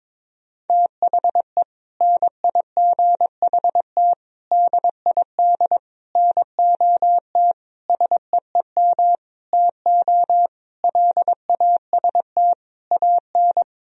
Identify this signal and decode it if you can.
Morse code, heard as a keyed tone, about 21 wpm, 700 Hz. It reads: THE NIGHT DID NOT SEEM TO LAST AN
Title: Sentences from Top 1000 Words - Review Part 1 - 22wpm